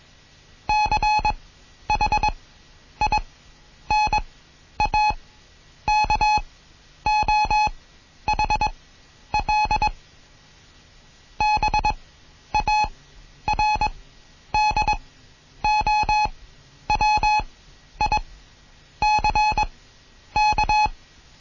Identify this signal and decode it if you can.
Morse code, heard as a keyed tone, about 21 wpm, spaced as for 10 wpm, 850 Hz. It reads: CHINAKOHL BARDOWICK
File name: Morsen / Kurs A 20